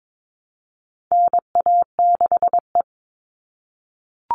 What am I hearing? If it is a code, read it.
Morse code, heard as a keyed tone, about 22 wpm, 700 Hz. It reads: NA6E
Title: All Cut Numbers - 4-Digit 22wpm